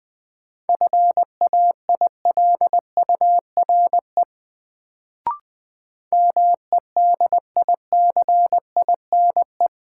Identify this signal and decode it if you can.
Morse code, heard as a keyed tone, about 20 wpm, 700 Hz. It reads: FAILURE MEDICINE
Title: Sentences from Top 900 Words - New Word Drill - 20wpm